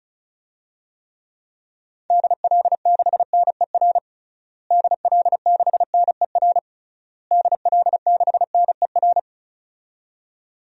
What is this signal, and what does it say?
Morse code, heard as a keyed tone, about 34 wpm, 700 Hz. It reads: DL6NER DL6NER DL6NER
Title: DXCC-100 Call Signs - 6 Characters - Repeated 3x - 35wpm